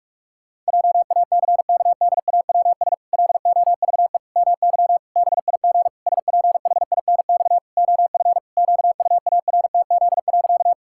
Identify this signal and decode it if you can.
Morse code, heard as a keyed tone, about 41 wpm, 700 Hz. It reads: JACKDAWS LOVE MY BIG SPHINX OF QUARTZ.
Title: Pangram Challenge v2 - 45wpm